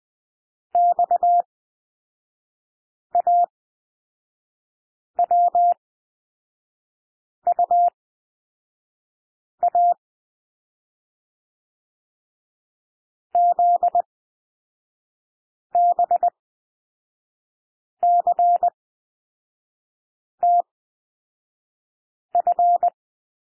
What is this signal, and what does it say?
Morse code, heard as a keyed tone, about 20 wpm, spaced as for 5 wpm, 700 Hz. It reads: XAWUA ZBCTF
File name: cwmp3 / L2 SOUND LETTERS 20 wpm